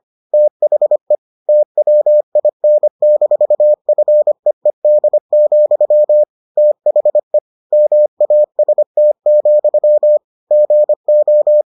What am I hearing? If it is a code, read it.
Morse code, heard as a keyed tone, about 25 wpm, 600 Hz. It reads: THE TWIN-FEED, THE MAST, GO